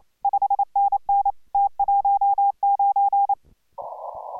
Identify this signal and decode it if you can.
Morse code, heard as a keyed tone, about 28 wpm, 810 Hz. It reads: 5NN T19